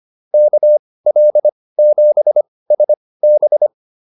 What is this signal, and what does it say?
Morse code, heard as a keyed tone, about 25 wpm, spaced as for 18 wpm, 600 Hz. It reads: KL7SB
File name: QSO107 18